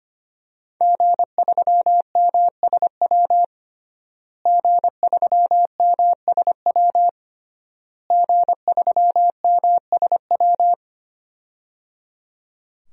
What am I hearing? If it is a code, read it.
Morse code, heard as a keyed tone, about 25 wpm, 700 Hz. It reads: G3MSW G3MSW G3MSW
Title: DXCC-100 Call Signs - 5 Characters - Repeated 3x - 25wpm